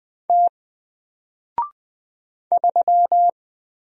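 Morse code, heard as a keyed tone, about 20 wpm, 700 Hz. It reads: T 3
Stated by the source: Single letter and number Morse drill at 20 wpm, from N4LCD.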